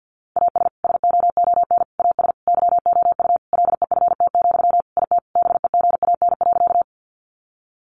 Morse code, heard as a keyed tone, about 50 wpm, 700 Hz. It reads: AS SOON AS YOU LEFT, IT BEGAN.
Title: Sentences from Top 400 Words 50wpm